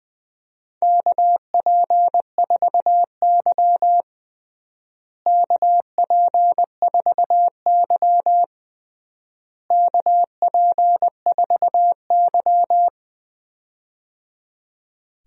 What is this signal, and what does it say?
Morse code, heard as a keyed tone, about 21 wpm, 700 Hz. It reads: KP4Y KP4Y KP4Y
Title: DXCC-100 Call Signs - 4 Characters - Repeated 3x - 20wpm